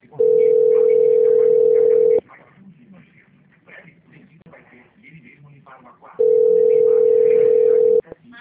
Ringback tone (North America), 440 + 480 Hz.